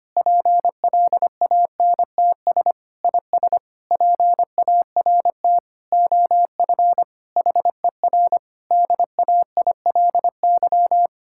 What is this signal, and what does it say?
Morse code, heard as a keyed tone, about 25 wpm, 700 Hz. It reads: PLANTS IS PART OF HER DAILY
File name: practice / 25wpm / Difficult Sentences with Hints - 25wpm